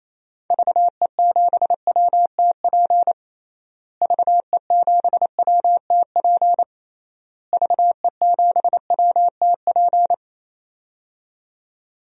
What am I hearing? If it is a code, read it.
Morse code, heard as a keyed tone, about 28 wpm, 700 Hz. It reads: VE7WTP VE7WTP VE7WTP
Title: DXCC-100 Call Signs - 6 Characters - Repeated 3x - 28wpm